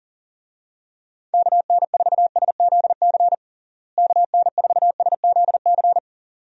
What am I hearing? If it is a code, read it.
Morse code, heard as a keyed tone, about 40 wpm, 700 Hz. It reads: KN4SZC KN4SZC
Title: USA Call Signs - 6 Characters - Repeated 2x - 40wpm